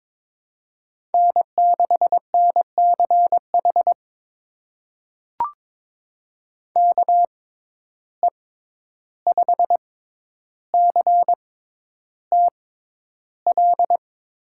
Morse code, heard as a keyed tone, about 22 wpm, spaced as for 11 wpm, 700 Hz. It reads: N6NCH KE5CTL
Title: USA Call Signs - 6x Character Spacing - 22wpm